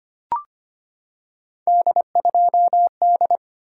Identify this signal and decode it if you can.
Morse code, heard as a keyed tone, about 25 wpm, 700 Hz. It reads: D2D
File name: Three Letter-Number Combinations - 25wpm